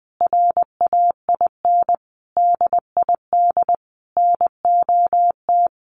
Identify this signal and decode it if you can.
Morse code, heard as a keyed tone, about 20 wpm, 700 Hz. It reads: RAIN DID NOT
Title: Sentences from Top 1300 Words 20wpm